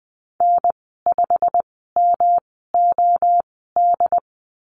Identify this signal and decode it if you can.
Morse code, heard as a keyed tone, about 20 wpm, spaced as for 14 wpm, 700 Hz. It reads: N5MOD
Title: 122 Lesson 20g - Number 3 - Call Signs 20wpm